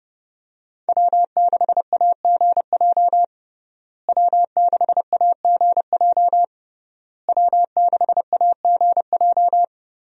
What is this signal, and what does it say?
Morse code, heard as a keyed tone, about 32 wpm, 700 Hz. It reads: W6AGJ W6AGJ W6AGJ
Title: USA Call Signs - 5 Characters - Repeated 3x - 30wpm